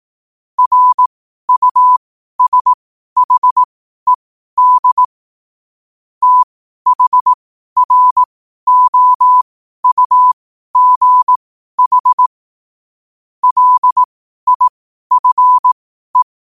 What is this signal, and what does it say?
Morse code, heard as a keyed tone, about 18 wpm, spaced as for 12 wpm, 1000 Hz. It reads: RUSHED THROUGH LIFE
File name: qotdfiles / 12wpm